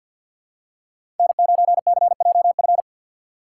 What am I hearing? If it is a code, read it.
Morse code, heard as a keyed tone, about 47 wpm, 700 Hz. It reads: N9CJF
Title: USA Call Signs - 5 Characters - 2x Character Spacing - 50wpm